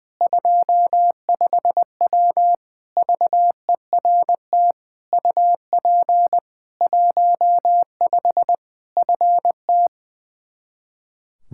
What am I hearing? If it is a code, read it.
Morse code, heard as a keyed tone, about 21 wpm, 700 Hz. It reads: 25W VERT UP 15 FT
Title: K1USN SST QSO Practice 20wpm